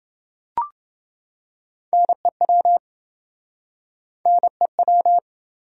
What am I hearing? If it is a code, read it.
Morse code, heard as a keyed tone, about 29 wpm, 700 Hz. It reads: NEW NEW